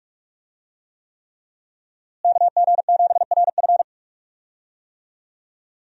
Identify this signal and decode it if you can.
Morse code, heard as a keyed tone, about 46 wpm, 700 Hz. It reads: KG7RF
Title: USA Call Signs - 5 Characters - Repeated 3x - 45wpm